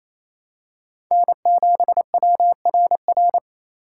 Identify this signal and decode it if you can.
Morse code, heard as a keyed tone, about 27 wpm, 700 Hz. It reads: N7WRR
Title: USA Call Signs - 5 Characters - 4x Character Spacing - 28wpm